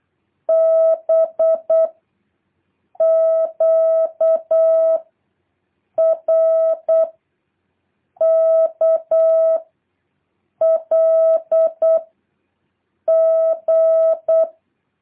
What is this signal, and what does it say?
Morse code, heard as a keyed tone, about 8 wpm, 650 Hz. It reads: B Q R K L G
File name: jh8cbh / 2023cw